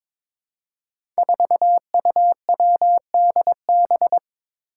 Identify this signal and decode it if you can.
Morse code, heard as a keyed tone, about 22 wpm, 700 Hz. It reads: 4UWDB
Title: Five Challenging Letter-Number Combinations 22wpm